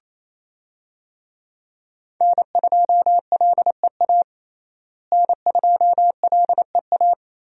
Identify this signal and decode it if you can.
Morse code, heard as a keyed tone, about 27 wpm, 700 Hz. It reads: N2LEA N2LEA
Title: USA Call Signs - 5 Characters - Repeated 2x - 28wpm